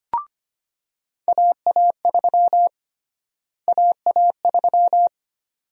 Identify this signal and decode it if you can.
Morse code, heard as a keyed tone, about 25 wpm, 700 Hz. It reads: AA3 AA3
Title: USA Call Sign Prefixes - Repeated 2x - 25wpm